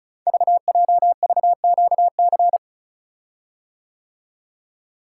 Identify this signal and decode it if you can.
Morse code, heard as a keyed tone, about 36 wpm, 700 Hz. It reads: VJVQC
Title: Code Groups - 5 Letters - 35wpm